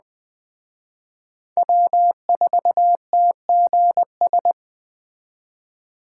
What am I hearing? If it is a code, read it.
Morse code, heard as a keyed tone, about 19 wpm, 700 Hz. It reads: W4TGS